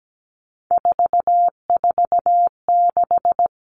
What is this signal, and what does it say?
Morse code, heard as a keyed tone, about 17 wpm, 700 Hz. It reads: 446
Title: Numbers - 3-Digit 17wpm